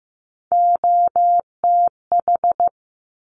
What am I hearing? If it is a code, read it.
Morse code, heard as a keyed tone, about 15 wpm, 700 Hz. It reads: OTH
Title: Top Three Letter Combinations in Words 15wpm